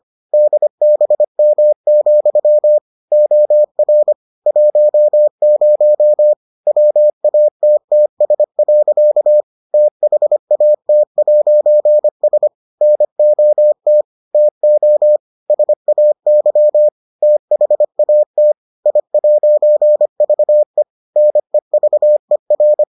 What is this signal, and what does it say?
Morse code, heard as a keyed tone, about 25 wpm, 600 Hz. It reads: DBM, OR 10 WATTS. THAT'S NOT TO SAY THAT I'VE NEVER